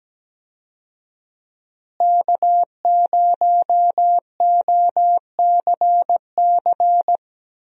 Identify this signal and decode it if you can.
Morse code, heard as a keyed tone, about 17 wpm, 700 Hz. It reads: K0OCC
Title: USA Call Signs - 5 Characters - Repeated 2x - 17wpm